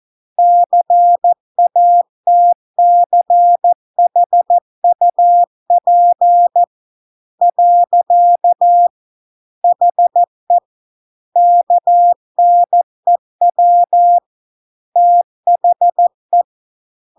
Morse code, heard as a keyed tone, about 14 wpm, 700 Hz. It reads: CATCHUP . HE KNEW THE